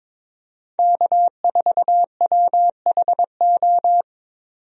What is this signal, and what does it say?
Morse code, heard as a keyed tone, about 22 wpm, 700 Hz. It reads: K4WHO
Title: USA Call Signs - 5 Characters - 22wpm